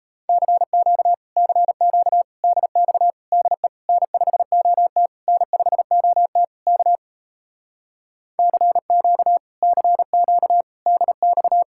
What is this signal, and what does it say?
Morse code, heard as a keyed tone, about 37 wpm, 700 Hz. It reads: CQ CQ DX DE N5OT N5OT K CQ CQ DX